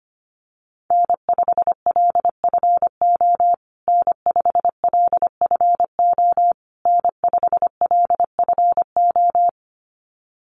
Morse code, heard as a keyed tone, about 25 wpm, 700 Hz. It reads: N5LFO N5LFO N5LFO
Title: USA Call Signs Repeated 3x 25wpm